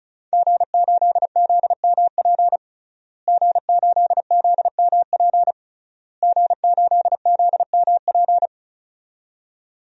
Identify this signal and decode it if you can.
Morse code, heard as a keyed tone, about 34 wpm, 700 Hz. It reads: G8ZMP G8ZMP G8ZMP
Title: DXCC-100 Call Signs - 5 Characters - Repeated 3x - 35wpm